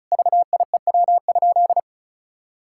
Morse code, heard as a keyed tone, about 36 wpm, 700 Hz. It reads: VIEW?
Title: Sentences from Top 1000 Words - Review Part 6 - 35wpm